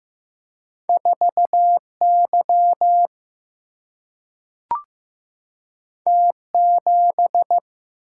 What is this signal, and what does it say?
Morse code, heard as a keyed tone, about 15 wpm, 700 Hz. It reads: 4 Y T 7
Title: Single-Letter Single-Number 15wpm